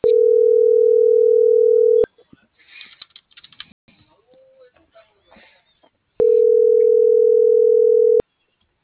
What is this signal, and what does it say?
Ringback tone (North America), 440 + 480 Hz.